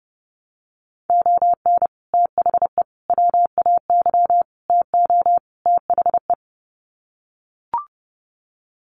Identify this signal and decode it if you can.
Morse code, heard as a keyed tone, about 28 wpm, 700 Hz. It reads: ON THE WAY TO THE
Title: Sets of 5 Words 30wpm